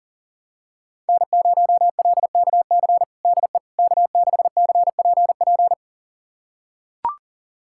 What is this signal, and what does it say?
Morse code, heard as a keyed tone, about 40 wpm, 700 Hz. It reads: N0LKC DE K6CPP
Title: USA Callsign DE Callsign - 5 Characters - 40wpm